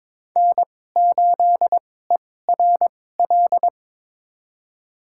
Morse code, heard as a keyed tone, about 22 wpm, spaced as for 16 wpm, 700 Hz. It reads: N8ERL
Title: USA Call Signs - 5 Characters - 2x Character Spacing - 22wpm